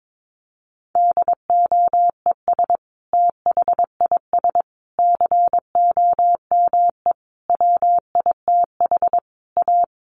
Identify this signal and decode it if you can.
Morse code, heard as a keyed tone, about 22 wpm, 700 Hz. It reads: DOES THIS COME WITH A
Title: Sentences from Top 1400 Words 22wpm